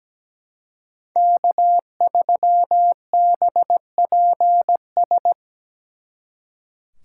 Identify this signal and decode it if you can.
Morse code, heard as a keyed tone, about 20 wpm, 700 Hz. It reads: K3BPS
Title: Speed Racing - USA 1x3 Call Signs 17wpm